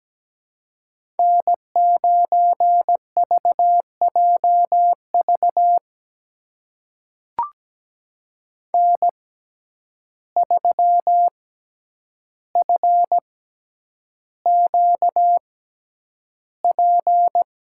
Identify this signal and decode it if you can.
Morse code, heard as a keyed tone, about 17 wpm, spaced as for 9 wpm, 700 Hz. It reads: N9VJV N3FQP
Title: USA Call Signs - 5 Characters - 6x Character Spacing - 17wpm